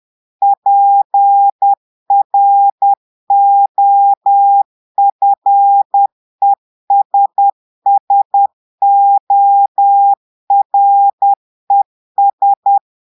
Morse code, heard as a keyed tone, about 10 wpm, 800 Hz. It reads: PROFESSORES
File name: Portugais - texte en clair 1 - vitesse - 10 mots minute